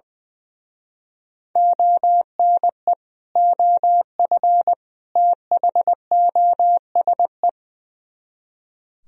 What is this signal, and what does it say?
Morse code, heard as a keyed tone, about 20 wpm, 700 Hz. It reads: ONE OF THOSE
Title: Speed-Racing - Sets of 3 Words 20wpm